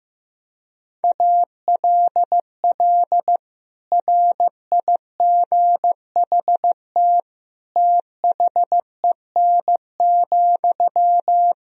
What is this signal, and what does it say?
Morse code, heard as a keyed tone, about 15 wpm, 700 Hz. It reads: ALL RIGHT THEN,
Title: Sentences from Top 500 Words and Abbreviations 15wpm